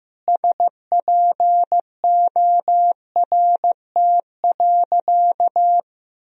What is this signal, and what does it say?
Morse code, heard as a keyed tone, about 15 wpm, 700 Hz. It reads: SPORT.
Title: Sentences from Top 1000 Words - Review Part 4 - 15wpm